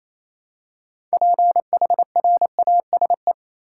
Morse code, heard as a keyed tone, about 28 wpm, 700 Hz. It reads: PHRASE
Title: Six Letter Words - 28wpm